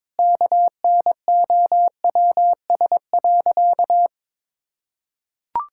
Morse code, heard as a keyed tone, about 22 wpm, 700 Hz. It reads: KNOWS.
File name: practice / 22wpm / Sentences with 4-Word Phrases - 22wpm